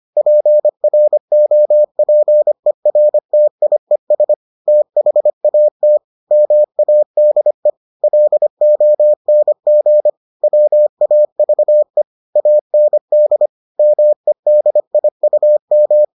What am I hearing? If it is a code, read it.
Morse code, heard as a keyed tone, about 25 wpm, 600 Hz. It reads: PROPERTIES THAT MADE LONG WAVE AND MEDIUM